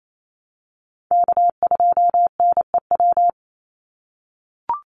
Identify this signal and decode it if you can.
Morse code, heard as a keyed tone, about 28 wpm, 700 Hz. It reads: K2NEW
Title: USA Call Signs 28wpm